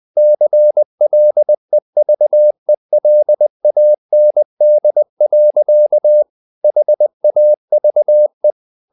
Morse code, heard as a keyed tone, about 20 wpm, 600 Hz. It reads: CLEVELAND. HAVE
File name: the old man and the sea / the old man and the sea part 1 20wpm